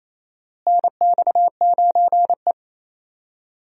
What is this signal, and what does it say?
Morse code, heard as a keyed tone, about 28 wpm, 700 Hz. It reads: NX9E
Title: USA Call Signs - 4 Characters - Repeated 3x - 28wpm